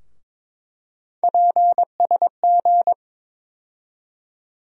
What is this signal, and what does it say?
Morse code, heard as a keyed tone, about 22 wpm, 700 Hz. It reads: PSG
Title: USA Call Sign Postfixes - 3 Letters 22wpm